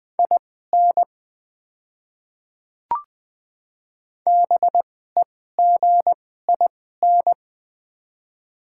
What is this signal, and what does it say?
Morse code, heard as a keyed tone, about 20 wpm, spaced as for 14 wpm, 700 Hz. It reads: IN BEGIN
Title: 2x Character Spacing - Top 300 Words - 20wpm